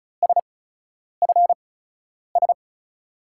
Morse code, heard as a keyed tone, about 36 wpm, spaced as for 10 wpm, 700 Hz. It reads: SFS
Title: USA Call Signs - 5 Characters - 8x Character Spacing - 35wpm